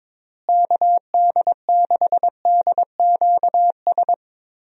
Morse code, heard as a keyed tone, about 22 wpm, 700 Hz. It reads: KD6DQS
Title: USA Call Signs - 6 Characters - Repeated 2x - 22wpm